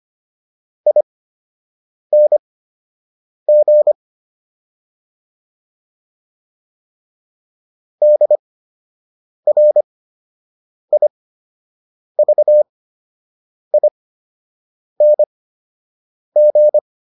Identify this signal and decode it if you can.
Morse code, heard as a keyed tone, about 25 wpm, spaced as for 7 wpm, 600 Hz. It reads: ING DRIVING